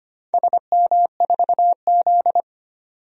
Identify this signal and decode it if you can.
Morse code, heard as a keyed tone, about 25 wpm, 690 Hz. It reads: SM4Z
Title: DXCC-100 Call Signs - 4 Characters - 25wpm